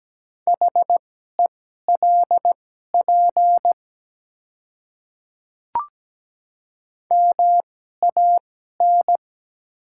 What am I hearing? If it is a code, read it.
Morse code, heard as a keyed tone, about 17 wpm, spaced as for 12 wpm, 690 Hz. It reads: HELP MAN
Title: 2x Character Spacing - Top 300 Words - 17wpm